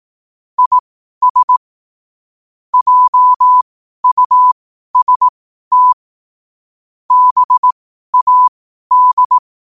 Morse code, heard as a keyed tone, about 18 wpm, spaced as for 12 wpm, 1000 Hz. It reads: IS JUST BAD
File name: qotdfiles / 12wpm